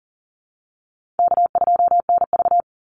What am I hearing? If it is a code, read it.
Morse code, heard as a keyed tone, about 40 wpm, 700 Hz. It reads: K2NV
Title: Easy Call Signs - 40wpm